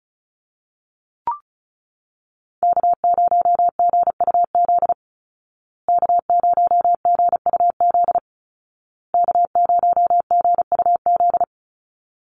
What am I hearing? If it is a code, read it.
Morse code, heard as a keyed tone, about 34 wpm, 700 Hz. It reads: K0GUZ K0GUZ K0GUZ
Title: Call Signs Repeated 2x - 35wpm